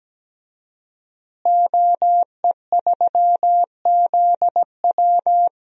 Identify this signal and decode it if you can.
Morse code, heard as a keyed tone, about 17 wpm, 700 Hz. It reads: OE3ZW
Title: DXCC-100 Call Signs - 5 Characters - Repeated 3x - 17wpm